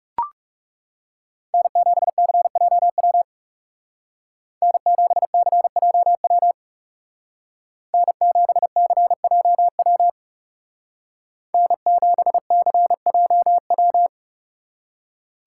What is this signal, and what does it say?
Morse code, heard as a keyed tone, about 37 wpm, 700 Hz. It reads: N7CJW N7CJW N7CJW N7CJW
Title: Speed Racing - USA 1x3 Call Signs 30wpm